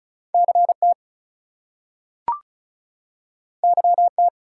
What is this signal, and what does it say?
Morse code, heard as a keyed tone, about 36 wpm, 700 Hz. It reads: CT YT
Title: US States and Canadian Provinces 35wpm